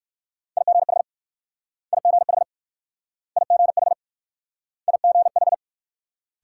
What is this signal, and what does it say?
Morse code, heard as a keyed tone, about 61 wpm, 700 Hz. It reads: IGH IGH IGH IGH